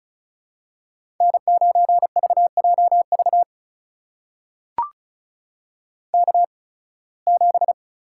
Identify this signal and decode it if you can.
Morse code, heard as a keyed tone, about 34 wpm, spaced as for 23 wpm, 700 Hz. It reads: N9VJV K7
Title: USA Call Signs - 5 Characters - 8x Character Spacing - 35wpm